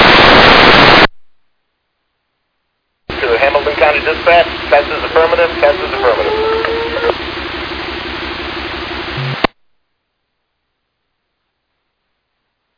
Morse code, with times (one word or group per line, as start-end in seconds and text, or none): none